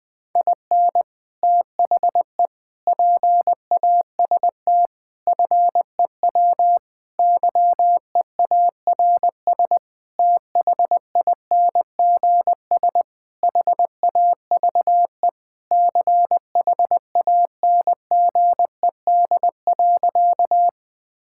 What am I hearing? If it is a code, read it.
Morse code, heard as a keyed tone, about 19 wpm, 700 Hz. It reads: IN THE PAST FEW YEARS THINGS HAVE CHANGED.
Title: Sentences with 5-Word Phrases - Spoken Beginning - 20wpm